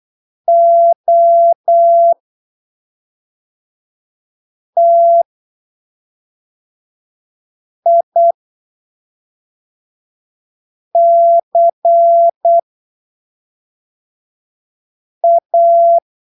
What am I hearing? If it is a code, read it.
Morse code, heard as a keyed tone, about 8 wpm, spaced as for 3 wpm, 680 Hz. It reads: OTICA
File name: arquivos / texto / texto 11 08ppm